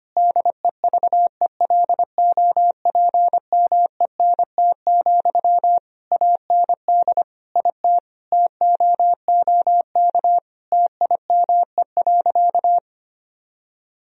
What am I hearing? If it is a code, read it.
Morse code, heard as a keyed tone, about 25 wpm, 700 Hz. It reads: DEVELOPMENT, AND IT TOOK TIME.
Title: Sentences from Top 1000 Words - Review Part 4 - 25wpm